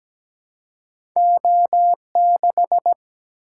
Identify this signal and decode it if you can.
Morse code, heard as a keyed tone, about 17 wpm, 700 Hz. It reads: O6
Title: Single-Letter Single-Number 17wpm